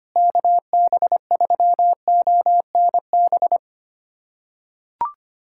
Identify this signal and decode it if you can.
Morse code, heard as a keyed tone, about 25 wpm, 700 Hz. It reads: KB3ONB
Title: USA Callsign DE Callsign - 6 Characters - 25wpm